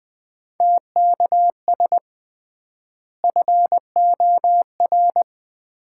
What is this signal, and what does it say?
Morse code, heard as a keyed tone, about 20 wpm, 700 Hz. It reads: TKS FOR
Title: Common QSO Phrases - 3x Word-Spacing - 20wpm